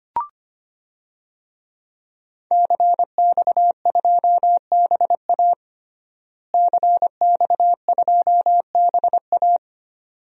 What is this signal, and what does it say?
Morse code, heard as a keyed tone, about 25 wpm, 700 Hz. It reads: CX2BA CX2BA
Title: DXCC-100 Call Signs - 5 Characters - Repeated 2x - 25wpm